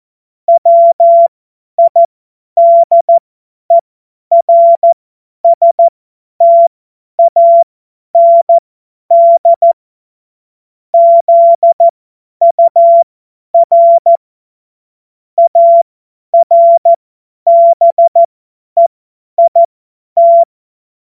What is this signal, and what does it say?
Morse code, heard as a keyed tone, about 14 wpm, spaced as for 10 wpm, 680 Hz. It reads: WIDERSTAND ZUR ARBEIT
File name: Morsekurs / Witze / tz 08. November 2022 50 F70